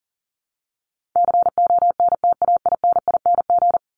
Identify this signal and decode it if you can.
Morse code, heard as a keyed tone, about 40 wpm, 700 Hz. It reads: CONTAINING
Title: ING Suffix - Top 100 Words with Base Words 40wpm